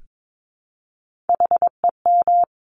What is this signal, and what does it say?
Morse code, heard as a keyed tone, about 22 wpm, 700 Hz. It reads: HEM
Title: Top Three Letter Combinations in Words 22wpm